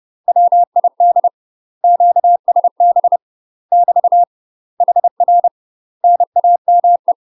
Morse code, heard as a keyed tone, about 30 wpm, 700 Hz. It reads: WID QSB <BT> HR NAME
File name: UK MASTER QSOs TRAINING / 30WPM / UK MASTER QSOs 0011